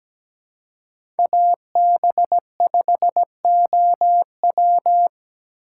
Morse code, heard as a keyed tone, about 21 wpm, 700 Hz. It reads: AB5OW